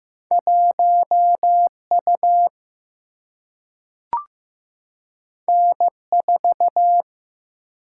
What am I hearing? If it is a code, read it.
Morse code, heard as a keyed tone, about 15 wpm, 700 Hz. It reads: 1 U N 4
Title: Single-Letter Single-Number 15wpm